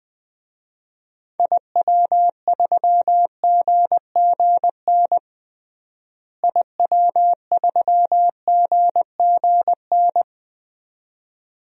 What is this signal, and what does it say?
Morse code, heard as a keyed tone, about 19 wpm, 700 Hz. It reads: IW3GGN IW3GGN
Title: DXCC-100 Call Signs - 6 Characters - Repeated 2x - 20wpm